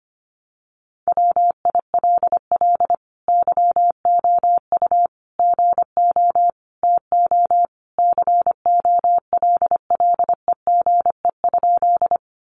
Morse code, heard as a keyed tone, about 25 wpm, 700 Hz. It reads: WILL YOU GO TO COLLEGE?
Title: Sentences from Top 1300 Words 25wpm